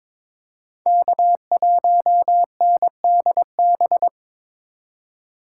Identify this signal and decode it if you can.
Morse code, heard as a keyed tone, about 22 wpm, 700 Hz. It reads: K1NDB
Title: USA Call Signs - 5 Characters - 2x Character Spacing - 22wpm